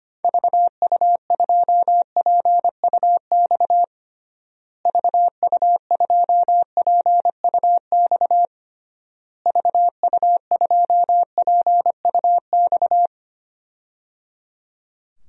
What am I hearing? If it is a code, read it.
Morse code, heard as a keyed tone, about 25 wpm, 700 Hz. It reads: VU2PUX VU2PUX VU2PUX
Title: DXCC-100 Call Signs - 6 Characters - Repeated 3x - 25wpm